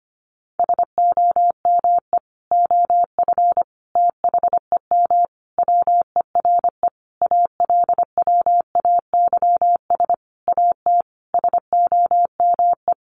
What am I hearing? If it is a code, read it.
Morse code, heard as a keyed tone, about 25 wpm, 700 Hz. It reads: SOME OF THEM WERE ALWAYS AT SOME
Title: Sentences from Top 1200 Words 25wpm